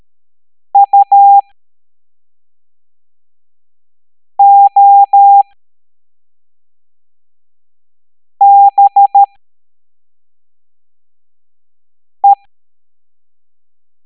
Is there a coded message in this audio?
Yes: Morse code from 0.7 to 12.3 s, heard as a keyed tone, about 13 wpm, spaced as for 3 wpm, 800 Hz. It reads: UOBE